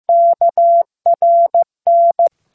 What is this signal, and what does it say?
Morse code, heard as a keyed tone, about 15 wpm, 680 Hz. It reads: KRN